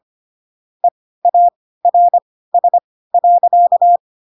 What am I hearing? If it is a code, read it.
Morse code, heard as a keyed tone, about 25 wpm, spaced as for 16 wpm, 700 Hz. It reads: EARS.
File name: Pinocchio / Pinocchio ch5 10 25 ik1hge